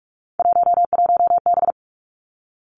Morse code, heard as a keyed tone, about 46 wpm, 700 Hz. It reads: 11B
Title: Three Challenging Letter-Number Combinations 45wpm